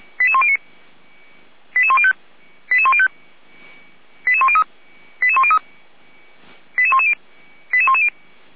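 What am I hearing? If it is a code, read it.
Five-tone selective call (ZVEI), heard as quick runs of five tones, tones of about 80 ms. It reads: 88109 88105 88105 88103 88103 88110 88110
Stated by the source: ZVEI five-tone call 88104 received by SDR, from feuerwehr-orlamuende fire brigade server.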